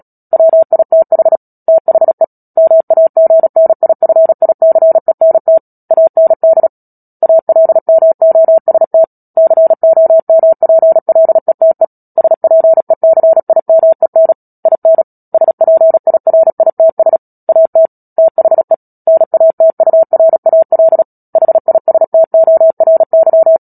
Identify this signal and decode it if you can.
Morse code, heard as a keyed tone, about 37 wpm, 650 Hz. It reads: WITH THE MAGNIFICENT AND ALMOST COMPLETE SPECIMEN IN SPIRITS AT THE NATURAL HISTORY
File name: Book Two / 10-Chapter 10 - Epilogue - 36 WPM